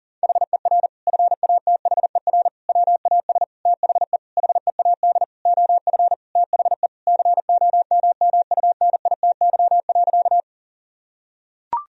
Morse code, heard as a keyed tone, about 40 wpm, 700 Hz. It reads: HER FATHER WAS THE HEAD OF THE COMMUNITY.
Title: Sentences from Top 1000 Words - Review Part 4 - 40wpm